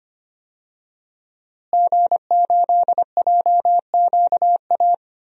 Morse code, heard as a keyed tone, about 25 wpm, 700 Hz. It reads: G8JQA
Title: DXCC-100 Call Signs - 5 Characters - Repeated 3x - 25wpm